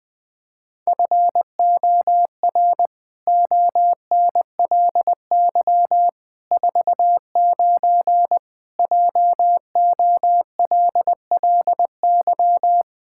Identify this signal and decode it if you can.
Morse code, heard as a keyed tone, about 20 wpm, 700 Hz. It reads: FOR ONLY 49 JOLLY
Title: Pangram Challenge v2 - 20wpm